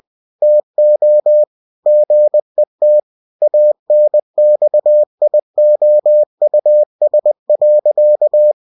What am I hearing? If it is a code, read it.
Morse code, heard as a keyed tone, about 20 wpm, 600 Hz. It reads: TO GET ANXIOUS.